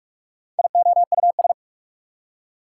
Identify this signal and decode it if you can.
Morse code, heard as a keyed tone, about 60 wpm, spaced as for 53 wpm, 690 Hz. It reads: IOUS
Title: Speed-Racing Prefixes and Suffixes 45wpm